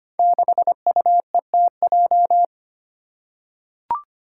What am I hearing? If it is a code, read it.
Morse code, heard as a keyed tone, about 25 wpm, 700 Hz. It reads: 6UETJ
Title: Five Challenging Letter-Number Combinations 25wpm